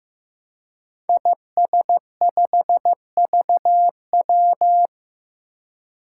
Morse code, heard as a keyed tone, about 15 wpm, 700 Hz. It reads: IS5VW
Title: Five Challenging Letter-Number Combinations 15wpm